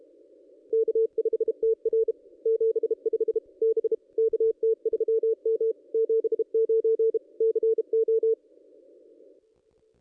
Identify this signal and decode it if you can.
Morse code, heard as a keyed tone, about 33 wpm, 450 Hz. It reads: K5TR 75 B KT3M 79 CO
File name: contests / ss-cw-k5tr-op-n5rz -1500